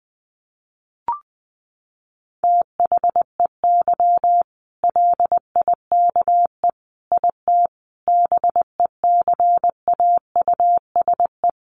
Morse code, heard as a keyed tone, about 20 wpm, 700 Hz. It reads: THEY LIKE IT BECAUSE
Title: Sentences from Top 600 Words 20wpm